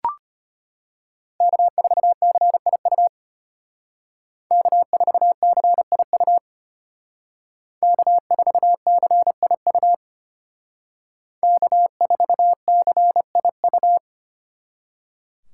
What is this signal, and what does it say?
Morse code, heard as a keyed tone, about 32 wpm, 700 Hz. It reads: K4CIU K4CIU K4CIU K4CIU